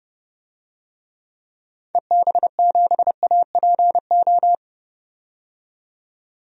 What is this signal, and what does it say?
Morse code, heard as a keyed tone, about 29 wpm, 690 Hz. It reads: EB7APO
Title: DXCC-100 Call Signs - 6 Characters - 30wpm